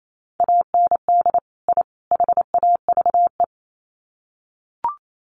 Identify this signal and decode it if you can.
Morse code, heard as a keyed tone, about 28 wpm, 700 Hz. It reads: AND I HAVE
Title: Sets of 3 Words 28wpm